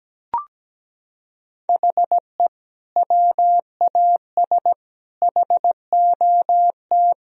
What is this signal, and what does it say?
Morse code, heard as a keyed tone, about 17 wpm, 700 Hz. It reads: HE WAS HOT
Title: Sentences from Top 1000 Words - Review Part 3 - 17wpm